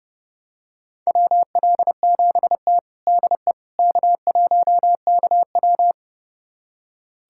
Morse code, heard as a keyed tone, about 32 wpm, 700 Hz. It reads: WL7T DE K1KW
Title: USA Callsign DE Callsign - 4 Characters - 30wpm